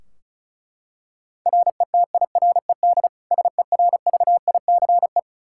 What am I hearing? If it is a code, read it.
Morse code, heard as a keyed tone, about 36 wpm, 700 Hz. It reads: RETIRED SERVICE
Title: K1USN SST QSO Practice 35wpm